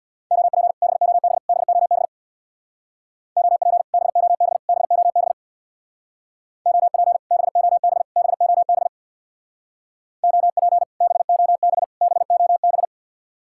Morse code, heard as a keyed tone, about 64 wpm, 700 Hz. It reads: OP BOB BOB OP BOB BOB OP BOB BOB OP BOB BOB